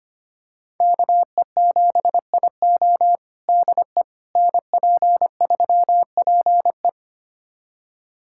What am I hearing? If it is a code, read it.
Morse code, heard as a keyed tone, about 25 wpm, 700 Hz. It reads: KE7IO DE NP3PE
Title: USA Callsign DE Callsign - 5 Characters - 25wpm